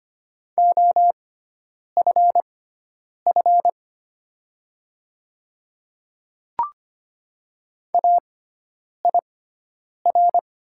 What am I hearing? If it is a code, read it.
Morse code, heard as a keyed tone, about 25 wpm, spaced as for 9 wpm, 690 Hz. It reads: OFF AIR
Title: 6x Character Spacing - Top 200 Words - 25wpm